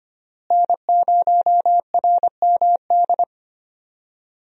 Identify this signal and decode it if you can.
Morse code, heard as a keyed tone, about 25 wpm, 700 Hz. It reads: N0RMD
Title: USA Call Signs - 5 Characters - Repeated 3x - 25wpm